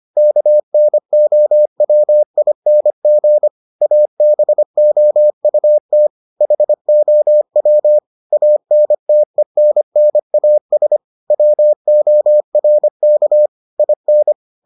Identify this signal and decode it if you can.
Morse code, heard as a keyed tone, about 25 wpm, 600 Hz. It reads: KNOWING ABOUT HOW ANTENNAS WORK IN